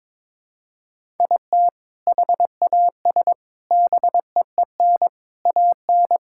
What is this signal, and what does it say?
Morse code, heard as a keyed tone, about 22 wpm, 700 Hz. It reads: IT HAS BEEN AN
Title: Sentences from Top 1000 Words - Review Part 5 - 22wpm